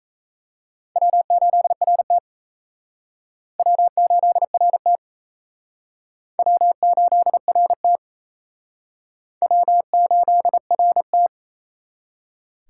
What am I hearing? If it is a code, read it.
Morse code, heard as a keyed tone, about 36 wpm, 700 Hz. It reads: W8RT W8RT W8RT W8RT